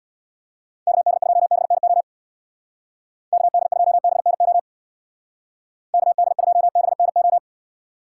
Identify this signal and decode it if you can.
Morse code, heard as a keyed tone, about 64 wpm, 700 Hz. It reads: KD2BNO KD2BNO KD2BNO